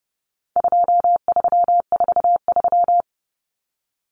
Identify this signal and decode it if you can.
Morse code, heard as a keyed tone, about 30 wpm, 700 Hz. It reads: 2343
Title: Numbers - 4-Digit 30wpm